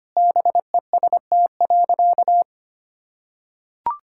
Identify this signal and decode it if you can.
Morse code, heard as a keyed tone, about 25 wpm, 700 Hz. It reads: BEST.
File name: practice / 25wpm / Sentences with 4-Word Phrases - 25wpm